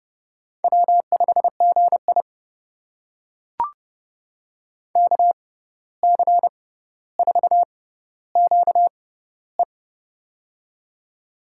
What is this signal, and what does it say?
Morse code, heard as a keyed tone, about 30 wpm, spaced as for 14 wpm, 700 Hz. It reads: W5GI KC4QE
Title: USA Call Signs - 6x Character Spacing - 30wpm